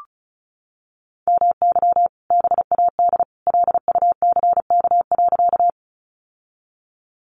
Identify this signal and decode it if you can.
Morse code, heard as a keyed tone, about 34 wpm, 700 Hz. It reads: MY BAD LUCK.
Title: Sentences from Top 1500 Words 35wpm